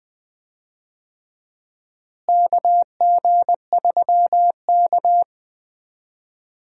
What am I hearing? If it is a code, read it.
Morse code, heard as a keyed tone, about 20 wpm, 700 Hz. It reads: KG3K
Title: USA Call Signs - 4 Characters - Repeated 2x - 20wpm